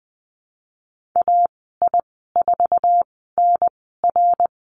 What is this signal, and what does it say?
Morse code, heard as a keyed tone, about 20 wpm, spaced as for 15 wpm, 700 Hz. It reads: AI4NR
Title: 055 Lesson 10g - Letter r - Call Signs 20wpm